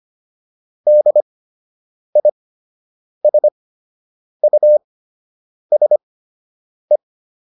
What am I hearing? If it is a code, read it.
Morse code, heard as a keyed tone, about 25 wpm, spaced as for 8 wpm, 600 Hz. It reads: DISUSE